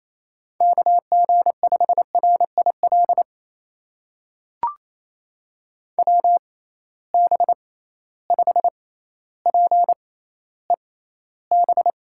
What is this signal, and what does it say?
Morse code, heard as a keyed tone, about 28 wpm, spaced as for 14 wpm, 700 Hz. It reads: KG5RIL WB5PEB
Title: USA Call Signs - 6 Characters - 6x Character Spacing - 28wpm